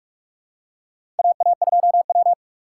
Morse code, heard as a keyed tone, about 46 wpm, 700 Hz. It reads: AA2W
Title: USA Call Signs - 4 Characters - 2x Character Spacing - 45wpm